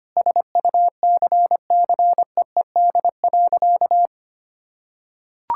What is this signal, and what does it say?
Morse code, heard as a keyed tone, about 25 wpm, 700 Hz. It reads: SUCCEED.
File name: practice / 25wpm / Sentences with 9-Word Phrases - 25wpm